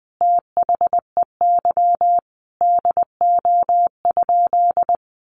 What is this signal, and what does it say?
Morse code, heard as a keyed tone, about 20 wpm, 700 Hz. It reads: THEY DO?
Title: Sentences from Top 100 Words 20wpm